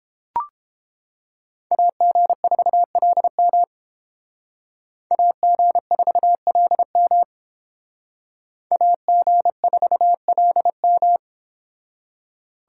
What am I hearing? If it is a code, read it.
Morse code, heard as a keyed tone, about 31 wpm, 700 Hz. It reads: AG4LM AG4LM AG4LM